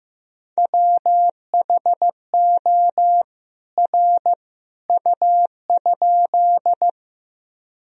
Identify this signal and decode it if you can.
Morse code, heard as a keyed tone, about 15 wpm, 700 Hz. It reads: WHO R U?
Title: Sentences from Top 100 Words and Abbreviations 15wpm